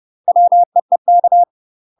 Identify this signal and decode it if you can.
Morse code, heard as a keyed tone, about 30 wpm, 700 Hz. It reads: WEEK
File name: UK MASTER QSOs TRAINING / 30WPM / UK MASTER QSOs 0007